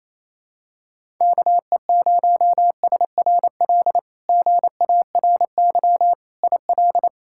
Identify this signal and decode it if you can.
Morse code, heard as a keyed tone, about 28 wpm, 700 Hz. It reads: KE0SRL GARY IL
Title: K1USN SST QSO Practice - 28wpm